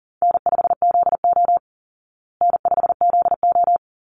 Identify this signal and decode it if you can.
Morse code, heard as a keyed tone, about 40 wpm, 700 Hz. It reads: N5ZO N5ZO
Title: Easy Call Signs Repeated 2x - 40wpm